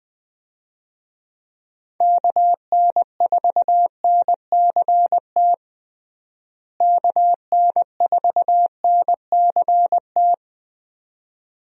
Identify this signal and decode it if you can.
Morse code, heard as a keyed tone, about 20 wpm, 700 Hz. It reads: KN4NCT KN4NCT
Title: USA Call Signs - 6 Characters - Repeated 2x - 20wpm